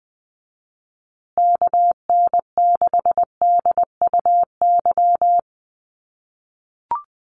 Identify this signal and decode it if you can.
Morse code, heard as a keyed tone, about 20 wpm, 700 Hz. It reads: KN6DUY
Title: USA Call Signs Repeated 3x 20wpm